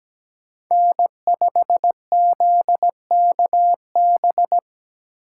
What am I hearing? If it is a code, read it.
Morse code, heard as a keyed tone, about 17 wpm, 700 Hz. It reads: N5ZKB
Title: USA Call Signs - 5 Characters - 4x Character Spacing - 17wpm